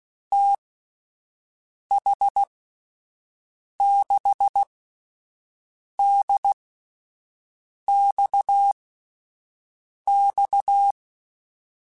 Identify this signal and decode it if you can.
Morse code, heard as a keyed tone, about 16 wpm, spaced as for 5 wpm, 800 Hz. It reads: TH6DXX
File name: (spaced) / QSOs Spaced-mp3s / WD0BGP 1 @10 sp